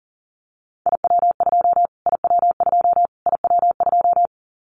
Morse code, heard as a keyed tone, about 40 wpm, 700 Hz. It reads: IW2 IW2 IW2
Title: Callsign Prefixes for Top 25 DXCC - Repeated 3x - 40wpm